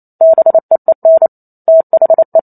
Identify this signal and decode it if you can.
Morse code, heard as a keyed tone, about 28 wpm, 650 Hz. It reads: BEEN THE